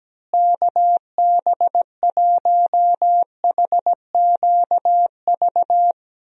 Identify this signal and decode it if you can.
Morse code, heard as a keyed tone, about 17 wpm, 700 Hz. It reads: KB1HQV
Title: USA Call Signs - 6 Characters - Repeated 2x - 17wpm